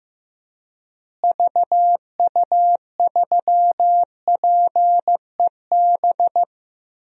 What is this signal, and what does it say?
Morse code, heard as a keyed tone, about 15 wpm, 700 Hz. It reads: VU3PEB
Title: DXCC-100 Call Signs - 6 Characters - Repeated 3x - 15wpm